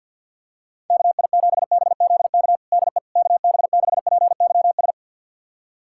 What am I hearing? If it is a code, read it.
Morse code, heard as a keyed tone, about 51 wpm, 700 Hz. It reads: KI7BZK DE KB6PYS
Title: USA Callsign DE Callsign - 6 Characters - 50wpm